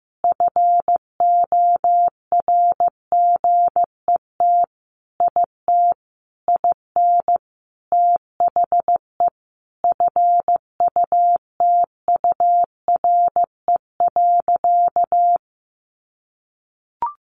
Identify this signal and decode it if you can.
Morse code, heard as a keyed tone, about 15 wpm, 700 Hz. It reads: FORGET IT IN THE FUTURE.
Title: Sentences from Top 1200 Words 15wpm